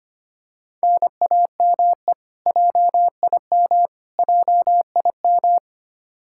Morse code, heard as a keyed tone, about 25 wpm, 700 Hz. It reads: NAME JIM JIM
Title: Practice QSO - Exchange 1 - Beginning - 25wpm